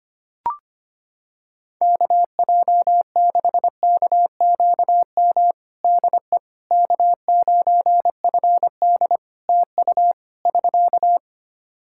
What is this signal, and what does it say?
Morse code, heard as a keyed tone, about 25 wpm, 700 Hz. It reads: KJ6KQM DE K9FD TU <SK>
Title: K1USN SST QSO Practice 25wpm